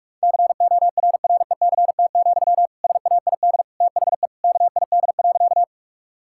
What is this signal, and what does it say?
Morse code, heard as a keyed tone, about 42 wpm, 700 Hz. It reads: CORRECT, SAID THE KID.
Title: Sentences from Top 1600 Words 45wpm